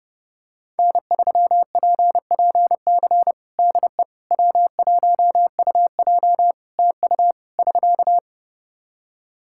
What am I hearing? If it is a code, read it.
Morse code, heard as a keyed tone, about 32 wpm, 700 Hz. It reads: N3PPC DE W1UJ TU <SK>
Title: Practice QSO - Exchange 4 - Ending - 30wpm